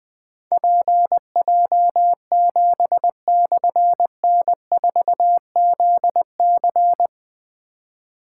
Morse code, heard as a keyed tone, about 21 wpm, 700 Hz. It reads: PJ7/N4ZC
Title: Difficult Call Signs 20wpm